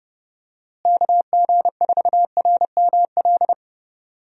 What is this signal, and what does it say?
Morse code, heard as a keyed tone, about 32 wpm, 700 Hz. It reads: KG4RML
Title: USA Call Signs - 6 Characters - Repeated 2x - 30wpm